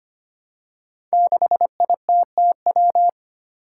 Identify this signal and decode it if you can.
Morse code, heard as a keyed tone, about 25 wpm, 700 Hz. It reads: 6ITTW
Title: Five Challenging Letter-Number Combinations 25wpm